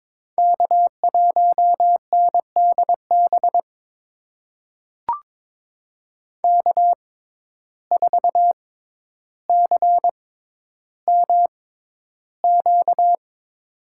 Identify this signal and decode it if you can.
Morse code, heard as a keyed tone, about 22 wpm, spaced as for 11 wpm, 700 Hz. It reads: K1NDB K4CMQ
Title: USA Call Signs - 5 Characters - 6x Character Spacing - 22wpm